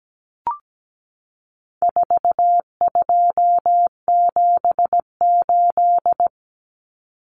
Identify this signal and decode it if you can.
Morse code, heard as a keyed tone, about 17 wpm, 700 Hz. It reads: 4278
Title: Numbers - 4-Digit 17wpm